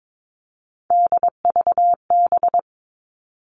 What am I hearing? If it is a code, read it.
Morse code, heard as a keyed tone, about 22 wpm, 700 Hz. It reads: DVB
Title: All Cut Numbers - 3-Digit 22wpm